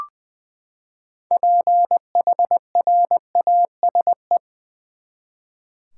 Morse code, heard as a keyed tone, about 20 wpm, 700 Hz. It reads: PHRASE
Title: Six Letter Words - 20wpm